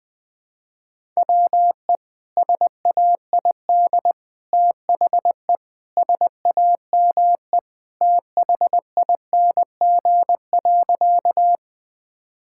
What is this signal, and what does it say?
Morse code, heard as a keyed tone, about 20 wpm, 700 Hz. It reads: WE SAID THE SAME THING.
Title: Sentences with 3-Word Phrases - Spoken Beginning - 20wpm